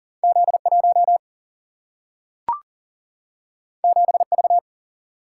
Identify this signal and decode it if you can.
Morse code, heard as a keyed tone, about 40 wpm, 700 Hz. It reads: Z 1 7 V
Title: Single-Letter Single-Number 40wpm